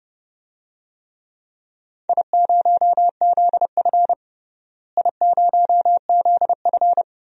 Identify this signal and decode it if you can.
Morse code, heard as a keyed tone, about 32 wpm, 700 Hz. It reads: I0ZF I0ZF
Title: DXCC-100 Call Signs - 4 Characters - Repeated 2x - 30wpm